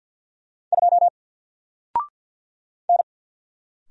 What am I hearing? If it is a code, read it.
Morse code, heard as a keyed tone, about 50 wpm, 700 Hz. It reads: 2 N
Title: Single Letter-Number 50wpm